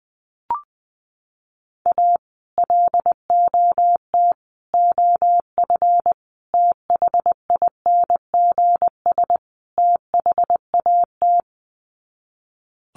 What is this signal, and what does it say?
Morse code, heard as a keyed tone, about 20 wpm, 700 Hz. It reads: A LOT OF THINGS THAT
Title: Sets of 5 Words 20wpm